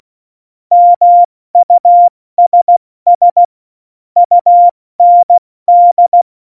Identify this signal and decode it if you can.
Morse code, heard as a keyed tone, about 16 wpm, 700 Hz. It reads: MUSS UND
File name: Morsekurs / Witze / Witz 03.Dezember 2019, 70 F80